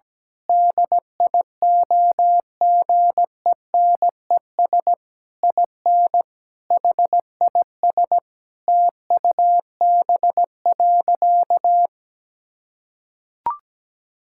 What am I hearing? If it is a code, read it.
Morse code, heard as a keyed tone, about 17 wpm, 700 Hz. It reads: DIOGENES IN HIS TUB.
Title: Pangram Challenge v2 - 17wpm